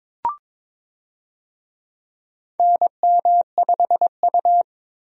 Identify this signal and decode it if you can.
Morse code, heard as a keyed tone, about 22 wpm, 700 Hz. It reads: NM5U
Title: USA Call Signs - 4 Characters - 22wpm